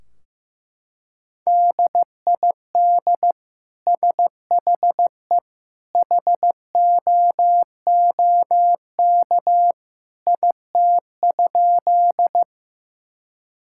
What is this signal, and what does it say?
Morse code, heard as a keyed tone, about 15 wpm, 700 Hz. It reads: DID SHE HOOK IT?
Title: Sentences from Top 1600 Words 15wpm